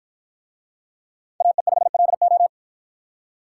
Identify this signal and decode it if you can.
Morse code, heard as a keyed tone, about 53 wpm, 700 Hz. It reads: AE5LO